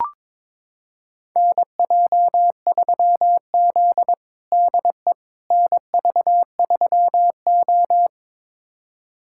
Morse code, heard as a keyed tone, about 21 wpm, 700 Hz. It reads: NJ3Z DE NV3O
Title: USA Callsign DE Callsign - 4 Characters - 22wpm